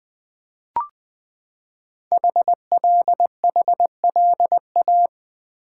Morse code, heard as a keyed tone, about 20 wpm, 700 Hz. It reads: HLHLA
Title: Code Groups - 5 Letters - 20wpm